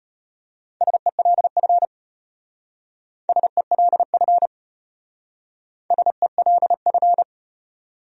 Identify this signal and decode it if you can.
Morse code, heard as a keyed tone, about 33 wpm, 700 Hz. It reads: SELF SELF SELF